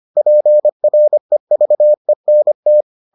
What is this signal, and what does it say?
Morse code, heard as a keyed tone, about 25 wpm, 600 Hz. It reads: PREVENT